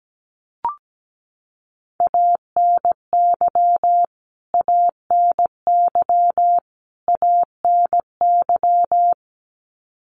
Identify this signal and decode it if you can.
Morse code, heard as a keyed tone, about 17 wpm, 700 Hz. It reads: ANY ANY ANY
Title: Top 500 Words Repeated 17wpm